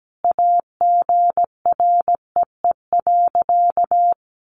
Morse code, heard as a keyed tone, about 17 wpm, 700 Hz. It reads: AGREE.
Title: Sentences from Top 1100 Words 17wpm